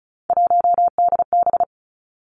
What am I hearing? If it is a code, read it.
Morse code, heard as a keyed tone, about 35 wpm, 700 Hz. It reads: 1DB
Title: Three Challenging Letter-Number Combinations 35wpm